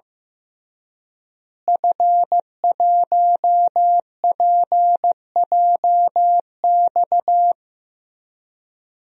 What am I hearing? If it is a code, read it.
Morse code, heard as a keyed tone, about 15 wpm, 700 Hz. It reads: F1PJX
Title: DXCC-100 Call Signs - 5 Characters - Repeated 3x - 15wpm